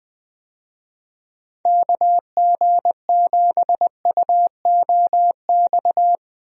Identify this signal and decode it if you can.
Morse code, heard as a keyed tone, about 20 wpm, 700 Hz. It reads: KG7UOX
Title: USA Call Signs - 6 Characters - Repeated 2x - 20wpm